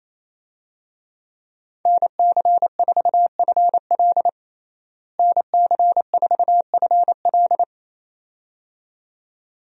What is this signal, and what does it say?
Morse code, heard as a keyed tone, about 28 wpm, 700 Hz. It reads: NC4FL NC4FL
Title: USA Call Signs - 5 Characters - Repeated 2x - 28wpm